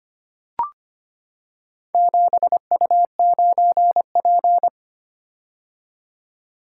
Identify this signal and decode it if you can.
Morse code, heard as a keyed tone, about 25 wpm, 700 Hz. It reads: 7U9P
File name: practice / 25wpm / Four Letter-Number Combinations - 25wpm